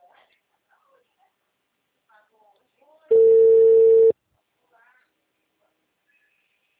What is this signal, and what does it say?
Ringback tone (China, 450 Hz), with speech.